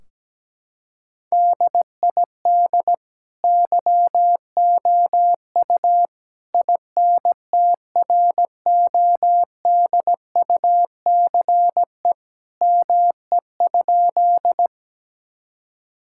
Morse code, heard as a keyed tone, about 17 wpm, 700 Hz. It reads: DID YOU INTRODUCE ME?
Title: Sentences from Top 1600 Words 17wpm